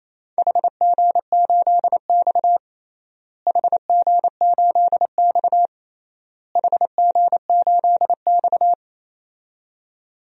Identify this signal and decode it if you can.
Morse code, heard as a keyed tone, about 28 wpm, 700 Hz. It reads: HG8X HG8X HG8X
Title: DXCC-100 Call Signs - 4 Characters - Repeated 3x - 28wpm